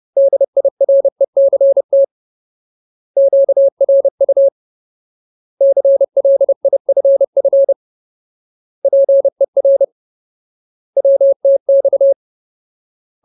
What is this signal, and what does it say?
Morse code, heard as a keyed tone, about 30 wpm, 550 Hz. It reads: DIRECT QRU CLIFF PER WTX
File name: Ultimate Word List / 30WPM Ultimate